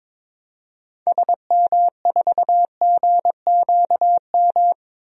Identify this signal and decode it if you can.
Morse code, heard as a keyed tone, about 22 wpm, 700 Hz. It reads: SM4GQM
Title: DXCC-100 Call Signs - 6 Characters - 22wpm